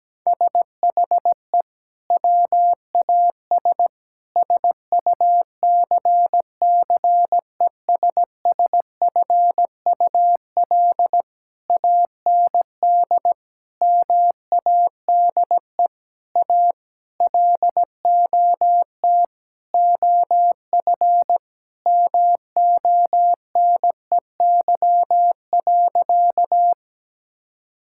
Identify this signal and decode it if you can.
Morse code, heard as a keyed tone, about 17 wpm, 700 Hz. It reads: SHE WAS SUCCESSFUL AND MADE A LOT OF MONEY.
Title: Sentences from Top 1000 Words - Review Part 6 - 17wpm